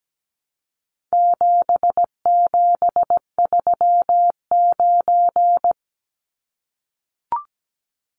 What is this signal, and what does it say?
Morse code, heard as a keyed tone, about 17 wpm, 700 Hz. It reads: 7739
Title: Numbers - 4-Digit 17wpm